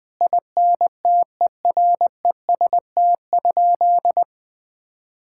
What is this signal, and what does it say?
Morse code, heard as a keyed tone, about 20 wpm, 700 Hz. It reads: INTEREST?
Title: Sentences from Top 1000 Words - Review Part 5 - 20wpm